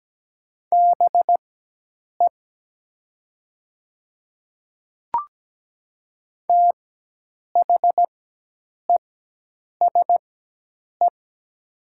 Morse code, heard as a keyed tone, about 17 wpm, spaced as for 8 wpm, 700 Hz. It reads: BE THESE
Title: 4x Character Spacing - Top 100 Words - 17wpm